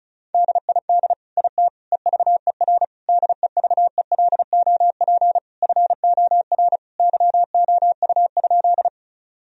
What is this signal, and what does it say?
Morse code, heard as a keyed tone, about 36 wpm, 700 Hz. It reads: DID IT EVER DEVELOP FOR YOU?
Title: Sentences from Top 1000 Words - Review Part 2 - 35wpm